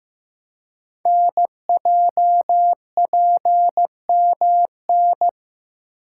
Morse code, heard as a keyed tone, about 15 wpm, 690 Hz. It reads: NJPMN
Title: Code Groups - 5 Letters - 15wpm